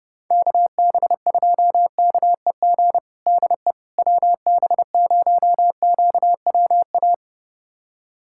Morse code, heard as a keyed tone, about 30 wpm, 700 Hz. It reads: KB2KEG DE WB0QWA
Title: USA Callsign DE Callsign 30wpm